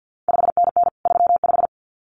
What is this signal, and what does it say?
Morse code, heard as a keyed tone, about 50 wpm, 700 Hz. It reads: 5NN 35
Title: CQ WW DX Contest Exchange - 50wpm